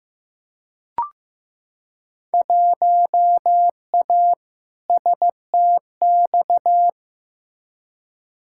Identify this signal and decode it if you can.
Morse code, heard as a keyed tone, about 15 wpm, 700 Hz. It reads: 1A STX
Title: Field Day Exchanges Encore - 15wpm